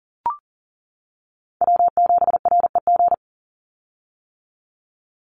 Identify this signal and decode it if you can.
Morse code, heard as a keyed tone, about 40 wpm, 700 Hz. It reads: W7REG
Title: USA Call Signs 40wpm